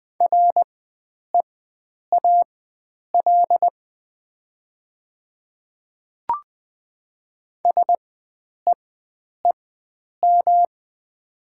Morse code, heard as a keyed tone, about 20 wpm, spaced as for 9 wpm, 700 Hz. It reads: REAL SEEM
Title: 4x Character Spacing - Top 300 Words - 20wpm